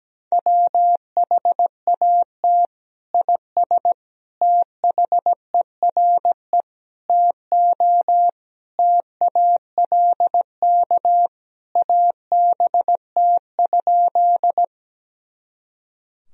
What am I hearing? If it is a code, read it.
Morse code, heard as a keyed tone, about 17 wpm, 700 Hz. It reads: WHAT IS THERE TO TALK ABT?
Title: Sentences from Top 400 Words and Abbreviations 17wpm